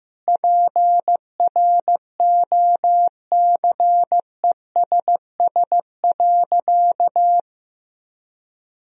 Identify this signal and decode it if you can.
Morse code, heard as a keyed tone, about 15 wpm, 700 Hz. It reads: PROCESS.
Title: Sentences from Top 1000 Words - Review Part 3 - 15wpm